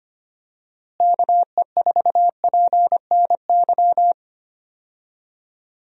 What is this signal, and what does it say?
Morse code, heard as a keyed tone, about 25 wpm, 700 Hz. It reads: KE4PNY
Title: USA Call Signs - 6 Characters - Repeated 3x - 25wpm